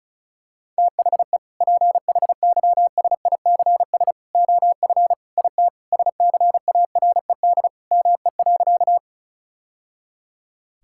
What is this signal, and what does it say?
Morse code, heard as a keyed tone, about 36 wpm, 700 Hz. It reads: THE PHYSICS OF IT SCARED ME.
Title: Sentences from Top 1000 Words - Review Part 5 - 35wpm